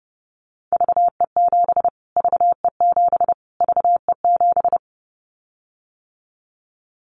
Morse code, heard as a keyed tone, about 30 wpm, 700 Hz. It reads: VE7 VE7 VE7
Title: Callsign Prefixes for Top 25 DXCC - Repeated 3x - 30wpm